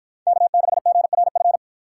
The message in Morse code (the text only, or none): K6GRF